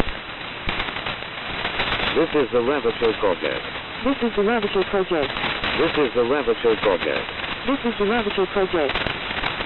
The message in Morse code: none